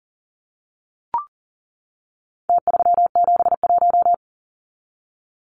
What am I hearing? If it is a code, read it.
Morse code, heard as a keyed tone, about 40 wpm, 700 Hz. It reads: T371
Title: Numbers - Contest Style - 4-Digit 40wpm